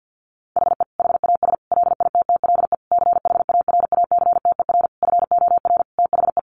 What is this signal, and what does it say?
Morse code, heard as a keyed tone, about 50 wpm, 700 Hz. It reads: HE HAS LITTLE CHARACTER FOR THE
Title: Sentences from Top 1200 Words 50wpm